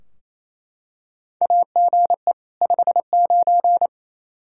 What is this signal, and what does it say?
Morse code, heard as a keyed tone, about 28 wpm, 700 Hz. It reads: AGE 59
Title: Practice QSO - Exchange 3 - More Info - 28wpm